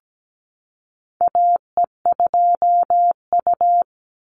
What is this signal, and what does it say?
Morse code, heard as a keyed tone, about 17 wpm, 700 Hz. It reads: AE2U
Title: Call Signs - 17wpm